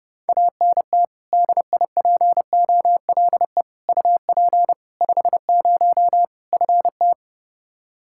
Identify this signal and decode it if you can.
Morse code, heard as a keyed tone, about 30 wpm, 700 Hz. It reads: ANT DIPOLE UP 50 FT
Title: Practice QSO - Exchange 2 - Info - 30wpm